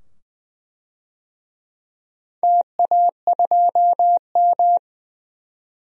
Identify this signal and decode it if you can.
Morse code, heard as a keyed tone, about 20 wpm, 700 Hz. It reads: TA2M
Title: DXCC-100 Call Signs - 4 Characters - 20wpm